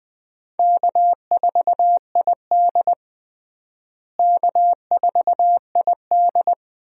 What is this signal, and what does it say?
Morse code, heard as a keyed tone, about 20 wpm, 700 Hz. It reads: K4ID K4ID
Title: USA Call Signs - 4 Characters - Repeated 2x - 20wpm